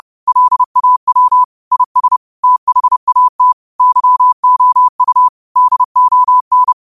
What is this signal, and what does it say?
Morse code, heard as a keyed tone, about 30 wpm, 1000 Hz. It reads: LAW IS THAT YOU DON
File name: qotdfiles / 30wpm